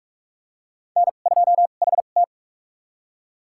Morse code, heard as a keyed tone, about 46 wpm, spaced as for 33 wpm, 700 Hz. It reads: N2HT
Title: USA Call Signs - 4 Characters - 2x Character Spacing - 45wpm